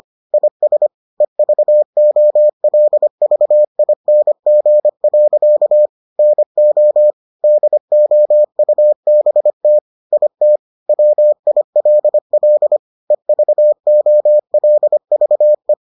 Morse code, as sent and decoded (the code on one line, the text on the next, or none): .. ... / . ...- --- .-.. ...- .. -. --. .-.-.- / -. --- / -.. --- ..- -... - / .. - / .-- .. .-.. .-.. / . ...- --- .-.. ...- .
IS EVOLVING. NO DOUBT IT WILL EVOLVE